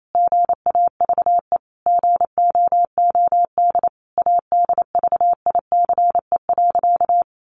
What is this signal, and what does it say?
Morse code, heard as a keyed tone, about 28 wpm, 700 Hz. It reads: GAVE GOOD ADVICE.
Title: Sentences from Top 1100 Words 28wpm